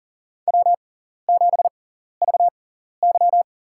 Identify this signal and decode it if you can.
Morse code, heard as a keyed tone, about 40 wpm, spaced as for 14 wpm, 700 Hz. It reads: W7VY
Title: USA Call Signs - 4 Characters - 6x Character Spacing - 40wpm